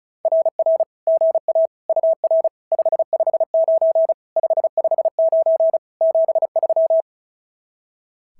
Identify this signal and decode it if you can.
Morse code, heard as a keyed tone, about 34 wpm, 650 Hz. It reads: RR GA UR 559 559 73
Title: SOTA QSO Practice 35wpm